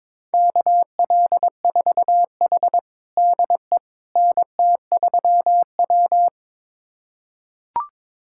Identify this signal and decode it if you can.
Morse code, heard as a keyed tone, about 22 wpm, 700 Hz. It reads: KL4H DE NT3W
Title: USA Callsign DE Callsign - 4 Characters - 22wpm